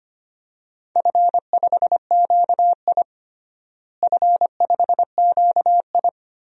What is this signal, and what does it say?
Morse code, heard as a keyed tone, about 25 wpm, 700 Hz. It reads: F5QI F5QI
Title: DXCC-100 Call Signs - 4 Characters - Repeated 3x - 25wpm